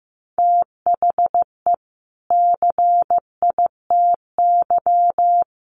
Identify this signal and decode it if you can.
Morse code, heard as a keyed tone, about 14 wpm, 700 Hz. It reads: THE CITY
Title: Sentences from Top 1300 Words 15wpm